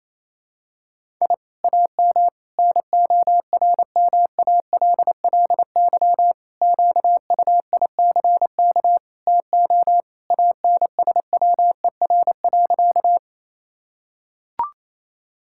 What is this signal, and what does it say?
Morse code, heard as a keyed tone, about 28 wpm, 700 Hz. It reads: I AM NORMALLY QUICK TO ANSWER.
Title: Sentences from Top 1000 Words - Review Part 6 - 28wpm